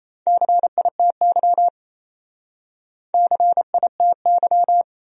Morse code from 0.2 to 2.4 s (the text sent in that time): CITY